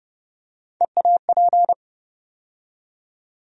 Morse code, heard as a keyed tone, about 30 wpm, 700 Hz. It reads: EAP
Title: USA Call Sign Postfixes - 3 Letters 30wpm